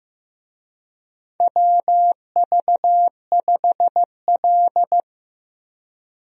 Morse code, heard as a keyed tone, about 15 wpm, 700 Hz. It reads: WV5L
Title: USA Call Signs - 4 Characters - Repeated 2x - 15wpm